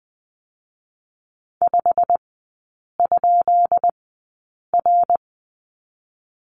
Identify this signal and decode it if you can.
Morse code, heard as a keyed tone, about 20 wpm, spaced as for 8 wpm, 700 Hz. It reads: 5?R
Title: 205 Lesson 33e - Letter j - Three Characters 20wpm